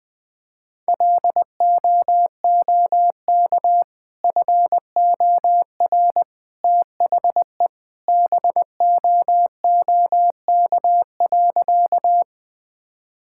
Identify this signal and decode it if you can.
Morse code, heard as a keyed tone, about 21 wpm, 700 Hz. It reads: LOOK FOR THE BOOK.
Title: Sentences with 2-Word Phrases - 20wpm